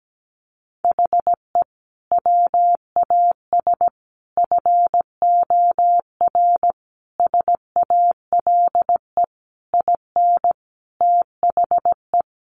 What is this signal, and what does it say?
Morse code, heard as a keyed tone, about 17 wpm, 700 Hz. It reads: HE WAS FOR SALE IN THE
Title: Sentences from Top 1400 Words 17wpm